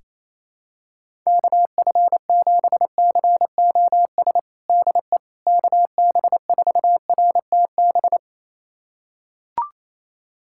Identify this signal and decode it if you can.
Morse code, heard as a keyed tone, about 28 wpm, 700 Hz. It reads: KF7COS DE KB4RTB
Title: USA Callsign DE Callsign - 6 Characters - 28wpm